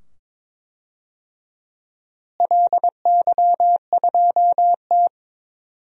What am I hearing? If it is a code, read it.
Morse code, heard as a keyed tone, about 22 wpm, 700 Hz. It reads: LY2T
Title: DXCC-100 Call Signs - 4 Characters - Repeated 2x - 22wpm